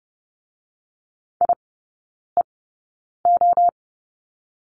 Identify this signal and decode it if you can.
Morse code, heard as a keyed tone, about 30 wpm, spaced as for 9 wpm, 690 Hz. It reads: IEO
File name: Lesson 17e - Letter u - Three Characters 30wpm